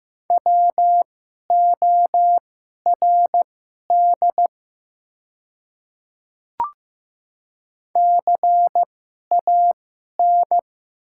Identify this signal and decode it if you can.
Morse code, heard as a keyed tone, about 15 wpm, spaced as for 11 wpm, 700 Hz. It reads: WORD CAN
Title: 2x Character Spacing - Top 100 Words - 15wpm